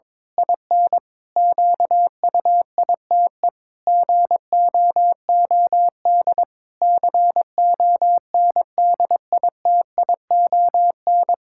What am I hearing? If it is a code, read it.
Morse code, heard as a keyed tone, about 22 wpm, 700 Hz. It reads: IN QUITE GOOD CONDITION
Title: Pangram Challenge v2 - 22wpm